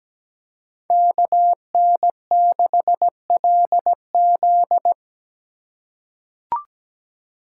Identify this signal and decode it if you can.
Morse code, heard as a keyed tone, about 17 wpm, 700 Hz. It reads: KN6LZ
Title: USA Call Signs - 2x Character Spacing - 17wpm